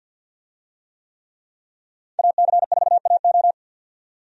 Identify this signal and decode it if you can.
Morse code, heard as a keyed tone, about 52 wpm, 700 Hz. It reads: AC4AO
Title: USA Call Signs - 5 Characters - 50wpm